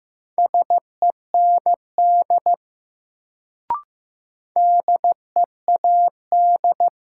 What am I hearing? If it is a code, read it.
Morse code, heard as a keyed tone, about 15 wpm, 700 Hz. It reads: SEND DEAD
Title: Sentences from Top 1500 Words - New Word Drill Alt - 15wpm